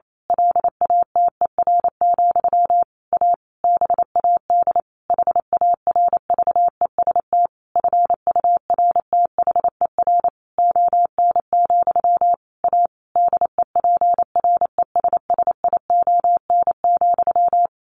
Morse code, heard as a keyed tone, about 28 wpm, 700 Hz. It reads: LATER, A BAD HARVEST FURTHER ON, A DEPRESSION,
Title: A Sound Of Thunder 28wpm